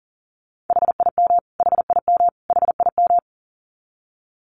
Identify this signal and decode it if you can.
Morse code, heard as a keyed tone, about 40 wpm, 700 Hz. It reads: HIM HIM HIM
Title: Top 300 Words Repeated 40wpm